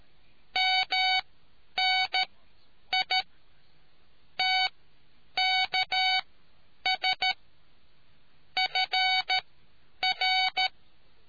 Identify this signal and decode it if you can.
Morse code, heard as a keyed tone, about 13 wpm, spaced as for 9 wpm, 750 Hz. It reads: MNI TKS FR